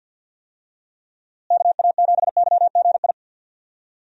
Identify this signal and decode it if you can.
Morse code, heard as a keyed tone, about 53 wpm, 700 Hz. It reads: KA7YGI
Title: USA Call Signs - 6 Characters - Repeated 3x - 50wpm